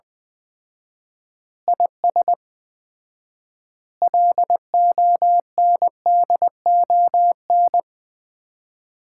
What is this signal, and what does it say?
Morse code, heard as a keyed tone, about 20 wpm, 700 Hz. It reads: IS LONDON
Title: Common QSO Phrases - 4x Word-Spacing - 20wpm